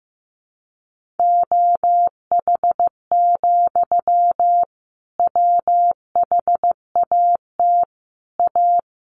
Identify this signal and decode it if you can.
Morse code, heard as a keyed tone, about 15 wpm, 700 Hz. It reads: OH, WHAT A
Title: Sentences from Top 1100 Words 15wpm